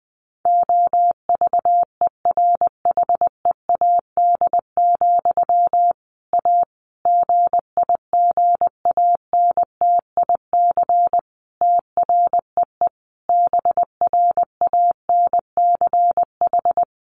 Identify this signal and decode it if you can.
Morse code, heard as a keyed tone, about 20 wpm, 700 Hz. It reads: OVERHEAD, A GIGANTIC TREE BRANCH
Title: A Sound Of Thunder 20wpm